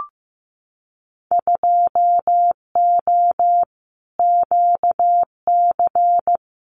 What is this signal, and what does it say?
Morse code, heard as a keyed tone, about 15 wpm, 700 Hz. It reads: 2O QC
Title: Winter Field Day Exchanges 15wpm